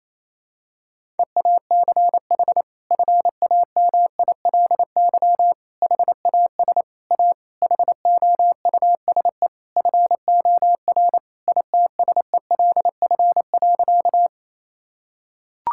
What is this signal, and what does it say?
Morse code, heard as a keyed tone, about 28 wpm, 700 Hz. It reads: EACH FAMILY HAS A HOUSE FOR ITSELF.
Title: Sentences from Top 1000 Words - Review Part 5 - 28wpm